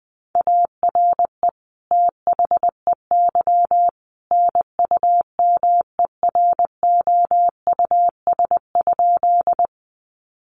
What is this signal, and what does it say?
Morse code, heard as a keyed tone, about 20 wpm, 700 Hz. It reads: ARE THEY NUMEROUS?
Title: Sentences from Top 1400 Words 20wpm